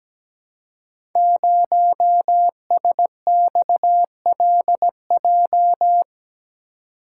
Morse code, heard as a keyed tone, about 17 wpm, 700 Hz. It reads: 0SXLJ
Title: Code Groups - 5 Letter:Numbers - 17wpm